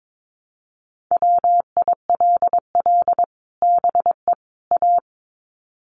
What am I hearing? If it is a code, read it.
Morse code, heard as a keyed tone, about 22 wpm, 700 Hz. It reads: WILL BE A
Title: Sets of 3 Words 22wpm